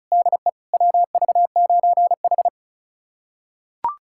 Morse code, heard as a keyed tone, about 36 wpm, 700 Hz. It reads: DE WV9H
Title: USA Callsign DE Callsign - 4 Characters - 35wpm